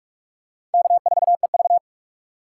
Morse code, heard as a keyed tone, about 41 wpm, 700 Hz. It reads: K4EV
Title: USA Call Signs - 4 Characters - 4x Character Spacing - 45wpm